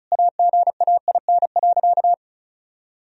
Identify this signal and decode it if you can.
Morse code, heard as a keyed tone, about 36 wpm, 700 Hz. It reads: AGAIN.
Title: Sentences from Top 1000 Words - Review Part 2 - 35wpm